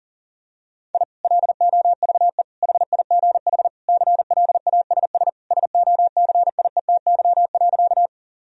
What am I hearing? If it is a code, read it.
Morse code, heard as a keyed tone, about 40 wpm, 700 Hz. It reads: I LOVE HIGH CLASS SOCIETY.
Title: Sentences from Top 1000 Words - Review Part 4 - 40wpm